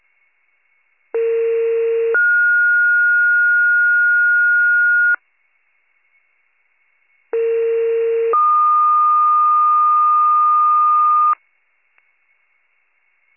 Two-tone page, heard as a one-second tone, then a three-second tone, about 460 then 1430 Hz, about 460 then 1220 Hz. It is heard from 1.1 to 11.3 s.